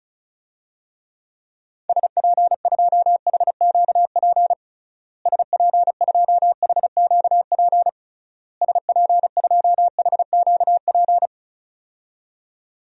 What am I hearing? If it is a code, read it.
Morse code, heard as a keyed tone, about 34 wpm, 700 Hz. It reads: SP2HQP SP2HQP SP2HQP
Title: DXCC-100 Call Signs - 6 Characters - Repeated 3x - 35wpm